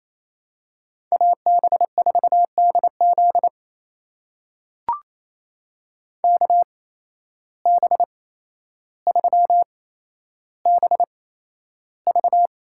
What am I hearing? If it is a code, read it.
Morse code, heard as a keyed tone, about 28 wpm, spaced as for 12 wpm, 700 Hz. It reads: AB4DZ KB3BV
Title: USA Call Signs - 5 Characters - 8x Character Spacing - 28wpm